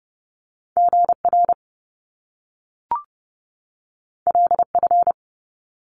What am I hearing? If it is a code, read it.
Morse code, heard as a keyed tone, about 30 wpm, 700 Hz. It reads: GR LF
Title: Top Two Letter Combinations in Words 30wpm